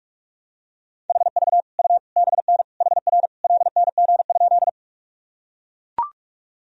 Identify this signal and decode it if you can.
Morse code, heard as a keyed tone, about 43 wpm, 700 Hz. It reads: HV U BN HR LNG?
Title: Sentences from Top 200 Words and Abbreviations 45wpm